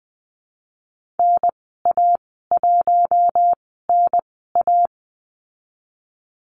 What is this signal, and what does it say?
Morse code, heard as a keyed tone, about 20 wpm, spaced as for 15 wpm, 700 Hz. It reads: NA1NA
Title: 042 Lesson 8g - Number 1 - Call Signs 20wpm